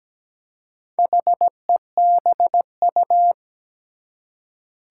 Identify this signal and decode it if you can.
Morse code, heard as a keyed tone, about 17 wpm, 700 Hz. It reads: HEBU
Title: Four Challenging Letter-Number Combinations 17wpm